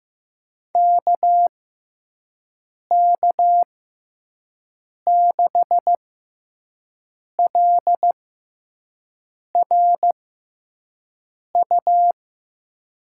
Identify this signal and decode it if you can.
Morse code, heard as a keyed tone, about 15 wpm, spaced as for 5 wpm, 700 Hz. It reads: KK6LRU
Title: USA Call Signs - 6 Characters - 6x Character Spacing - 15wpm